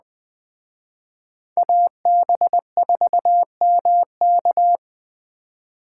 Morse code, heard as a keyed tone, about 20 wpm, 700 Hz. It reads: AB4MK
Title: Speed Racing - USA Call Signs 17wpm